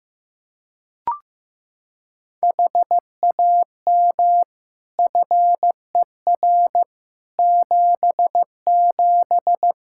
Morse code, heard as a keyed tone, about 15 wpm, 700 Hz. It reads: HAM FER 77
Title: K1USN SST QSO Practice 15wpm